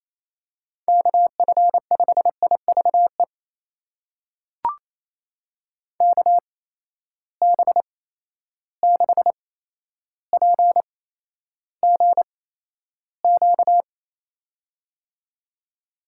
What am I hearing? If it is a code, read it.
Morse code, heard as a keyed tone, about 27 wpm, spaced as for 12 wpm, 700 Hz. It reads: KF5IVE KB6PGQ
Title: USA Call Signs - 6 Characters - 8x Character Spacing - 28wpm